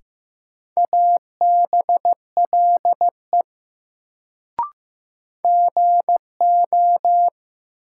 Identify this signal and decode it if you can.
Morse code, heard as a keyed tone, about 15 wpm, 700 Hz. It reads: ABLE GO
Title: Sets of 2 Words - New Word Drill - 15wpm